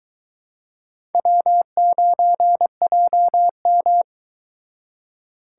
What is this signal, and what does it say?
Morse code, heard as a keyed tone, about 22 wpm, 700 Hz. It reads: W9JM